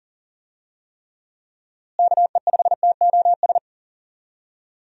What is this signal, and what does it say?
Morse code, heard as a keyed tone, about 40 wpm, 700 Hz. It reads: KE5TOS
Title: USA Call Signs - 6 Characters - Repeated 2x - 40wpm